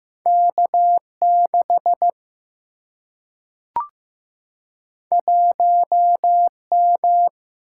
Morse code, heard as a keyed tone, about 15 wpm, 700 Hz. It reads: K6 1M
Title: Single-Letter Single-Number 15wpm